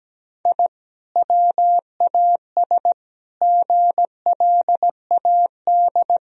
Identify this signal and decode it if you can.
Morse code, heard as a keyed tone, about 17 wpm, 700 Hz. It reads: I WAS GLAD
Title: Sentences from Top 1000 Words - Review Part 1 - 17wpm